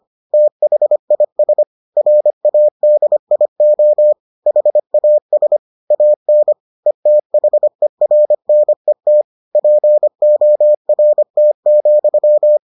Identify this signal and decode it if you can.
Morse code, heard as a keyed tone, about 25 wpm, 600 Hz. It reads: THIS RADIO HAS AN ETHERNET PORT,